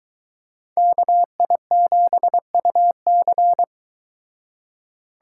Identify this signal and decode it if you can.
Morse code, heard as a keyed tone, about 22 wpm, 700 Hz. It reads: KI7UC